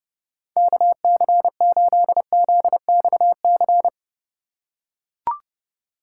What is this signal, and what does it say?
Morse code, heard as a keyed tone, about 32 wpm, 700 Hz. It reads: KC8ZXC
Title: USA Call Signs - 4x Character Spacing - 30wpm